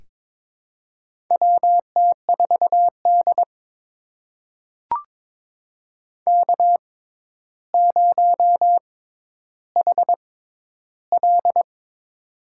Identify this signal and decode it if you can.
Morse code, heard as a keyed tone, about 22 wpm, spaced as for 11 wpm, 700 Hz. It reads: WT4D K0HL
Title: USA Call Signs - 4 Characters - 6x Character Spacing - 22wpm